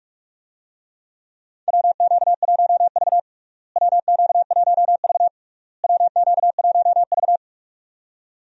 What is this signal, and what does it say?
Morse code, heard as a keyed tone, about 41 wpm, 700 Hz. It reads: WQ1V WQ1V WQ1V
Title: USA Call Signs - 4 Characters - Repeated 3x - 45wpm